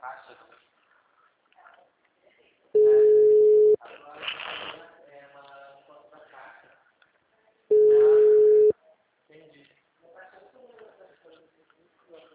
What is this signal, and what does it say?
Ringback tone (Europe, 425 Hz), with speech.
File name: out- -2010- - - .52